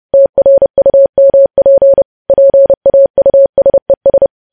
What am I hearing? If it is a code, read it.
Morse code, heard as a keyed tone, about 30 wpm, 550 Hz. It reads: TRUMP PAUSES
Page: news headlines in morse code at 30 wpm